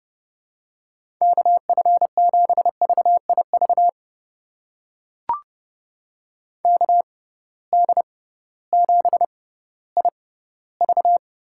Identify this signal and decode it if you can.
Morse code, heard as a keyed tone, about 30 wpm, spaced as for 16 wpm, 700 Hz. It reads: KF7VIV KD7IV
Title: USA Call Signs - 6 Characters - 6x Character Spacing - 30wpm